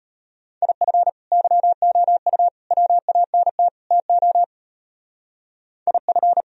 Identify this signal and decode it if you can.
Morse code, heard as a keyed tone, about 31 wpm, 700 Hz. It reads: IF YOU WANT TO IF
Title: Speed-Racing - Sets of 4 Words 25wpm